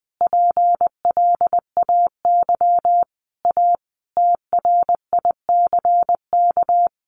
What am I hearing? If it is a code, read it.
Morse code, heard as a keyed tone, about 20 wpm, 700 Hz. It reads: PLAY A TRICK
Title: Sentences from Top 1500 Words 20wpm